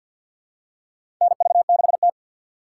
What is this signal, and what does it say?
Morse code, heard as a keyed tone, about 53 wpm, 700 Hz. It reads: NV6T
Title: USA Call Signs - 4 Characters - 6x Character Spacing - 50wpm